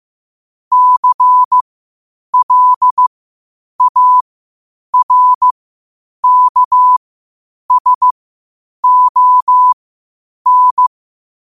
Morse code, heard as a keyed tone, about 15 wpm, spaced as for 8 wpm, 1000 Hz. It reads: CLARKSON
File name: qotdfiles / 8wpm